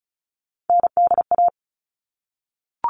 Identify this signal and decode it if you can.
Morse code, heard as a keyed tone, about 34 wpm, 700 Hz. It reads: NDA
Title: All Cut Numbers - 3-Digit 35wpm